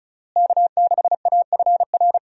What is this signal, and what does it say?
Morse code, heard as a keyed tone, about 36 wpm, 700 Hz. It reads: K6AFR
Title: USA Call Signs - 5 Characters - Repeated 3x - 35wpm